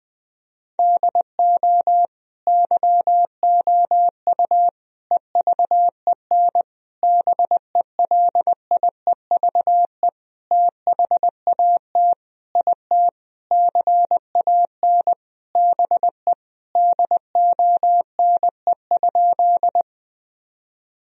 Morse code, heard as a keyed tone, about 20 wpm, 700 Hz. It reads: DO YOU EVEN BELIEVE THAT IT CAN BE DONE?
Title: Sentences from Top 1000 Words - Review Part 1 - 20wpm